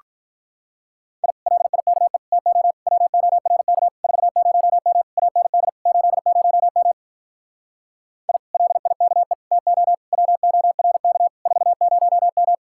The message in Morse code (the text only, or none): I LIKE TO WORK 40M AND 80M I LIKE TO WORK 40M